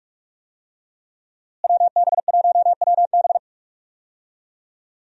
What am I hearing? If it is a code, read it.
Morse code, heard as a keyed tone, about 45 wpm, 700 Hz. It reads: WB1WB
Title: USA Call Signs - 5 Characters - Repeated 2x - 45wpm